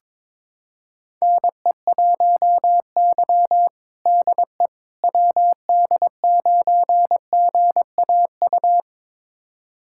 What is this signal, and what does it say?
Morse code, heard as a keyed tone, about 22 wpm, 700 Hz. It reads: NE1Y DE WD9GAU
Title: USA Callsign DE Callsign 22wpm